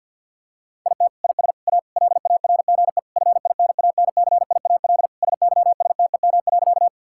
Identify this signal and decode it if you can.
Morse code, heard as a keyed tone, about 51 wpm, 690 Hz. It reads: IT IS A LARGE FINANCIAL SYSTEM.
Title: Sentences from Top 1000 Words - Review Part 5 - 50wpm